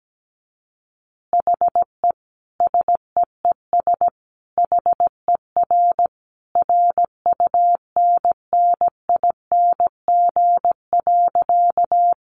Morse code, heard as a keyed tone, about 17 wpm, 700 Hz. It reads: HE SEES HER RUNNING.
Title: Additional Easy Phrases 17wpm